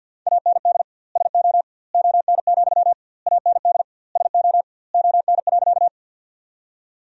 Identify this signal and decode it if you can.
Morse code, heard as a keyed tone, about 50 wpm, 700 Hz. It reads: AND SO ON, AND SO ON.
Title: 2x Word-Spacing - Sentences from Top 100 Words - 50wpm